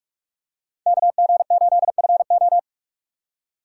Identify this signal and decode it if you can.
Morse code, heard as a keyed tone, about 44 wpm, 700 Hz. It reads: KG8FO
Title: USA Call Signs - 5 Characters - 6x Character Spacing - 45wpm